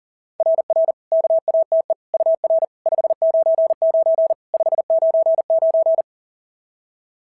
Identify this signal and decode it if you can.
Morse code, heard as a keyed tone, about 39 wpm, 650 Hz. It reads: RR KATE UR 599 599
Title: SOTA QSO Practice 40wpm